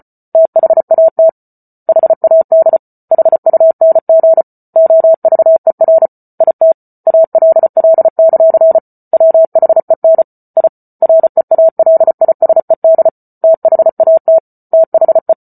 Morse code, heard as a keyed tone, about 34 wpm, 650 Hz. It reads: THAT HAD HUNG OVER IT ALL; WHEN I REALISED THAT THE